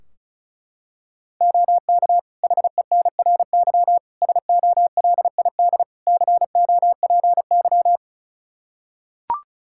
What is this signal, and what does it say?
Morse code, heard as a keyed tone, about 34 wpm, 700 Hz. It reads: OK HENRY SOLID COPY
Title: Practice QSO - Exchange 3 - More Info - 35wpm